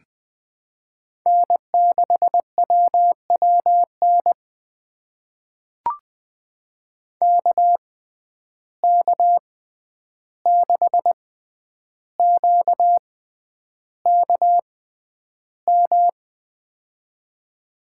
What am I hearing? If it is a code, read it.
Morse code, heard as a keyed tone, about 21 wpm, spaced as for 10 wpm, 700 Hz. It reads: N6WWN KK6QKM
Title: USA Call Signs - 6x Character Spacing - 20wpm